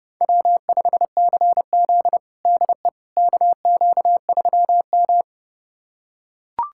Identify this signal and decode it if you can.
Morse code, heard as a keyed tone, about 30 wpm, 700 Hz. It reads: W5CZ DE KQ3M
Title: USA Callsign DE Callsign - 4 Characters - 30wpm